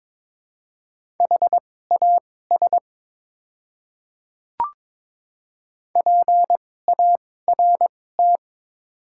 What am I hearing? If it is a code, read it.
Morse code, heard as a keyed tone, about 22 wpm, spaced as for 16 wpm, 700 Hz. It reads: HAS PART
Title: 2x Character Spacing - Top 500 Words - 22wpm